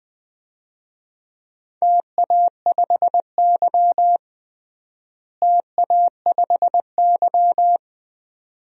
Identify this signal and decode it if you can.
Morse code, heard as a keyed tone, about 21 wpm, 700 Hz. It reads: TA5Y TA5Y
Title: DXCC-100 Call Signs - 4 Characters - Repeated 2x - 20wpm